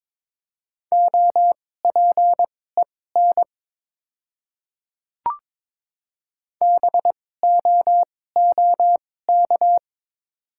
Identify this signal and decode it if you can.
Morse code, heard as a keyed tone, about 22 wpm, spaced as for 16 wpm, 690 Hz. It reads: OPEN BOOK
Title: 2x Character Spacing - Top 300 Words - 22wpm